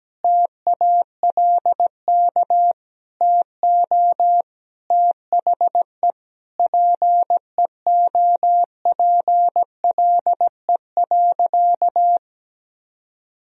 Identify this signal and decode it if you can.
Morse code, heard as a keyed tone, about 17 wpm, 700 Hz. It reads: TALK TO THE PEOPLE.
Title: Sentences from Top 400 Words and Abbreviations 17wpm